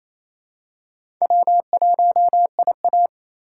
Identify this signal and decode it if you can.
Morse code, heard as a keyed tone, about 28 wpm, 700 Hz. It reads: W1IA
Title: USA Call Signs - 4 Characters - 4x Character Spacing - 28wpm